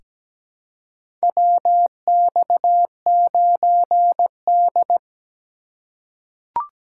Morse code, heard as a keyed tone, about 17 wpm, 700 Hz. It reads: WX9D
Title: USA Call Signs - 4 Characters - 3x Character Spacing - 17wpm